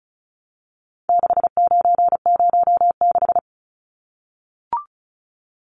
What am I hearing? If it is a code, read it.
Morse code, heard as a keyed tone, about 34 wpm, 700 Hz. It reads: 6906
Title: Numbers - 4-Digit 35wpm